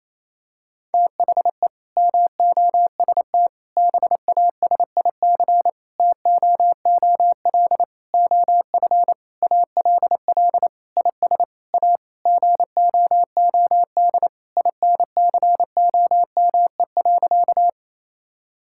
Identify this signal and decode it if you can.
Morse code, heard as a keyed tone, about 28 wpm, 700 Hz. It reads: THE MOST BASIC TOOL OF ALL IS A GOOD INCOME.
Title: Sentences from Top 1000 Words - Review Part 6 - 28wpm